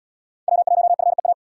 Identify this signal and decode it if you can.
Morse code, heard as a keyed tone, about 77 wpm, 700 Hz. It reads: W1FEA